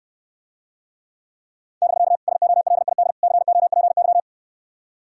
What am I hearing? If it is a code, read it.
Morse code, heard as a keyed tone, about 69 wpm, 700 Hz. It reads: <BK> SOLID COPY